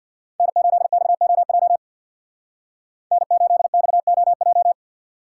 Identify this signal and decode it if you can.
Morse code, heard as a keyed tone, about 64 wpm, 700 Hz. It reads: N8XOJ N8XOJ